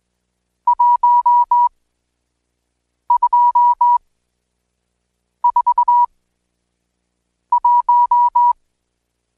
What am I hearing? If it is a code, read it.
Morse code, heard as a keyed tone, about 20 wpm, spaced as for 5 wpm, 970 Hz. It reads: 1241
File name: Code Quick / AudioDownld1 / Seg76-Day8-DayDrill 8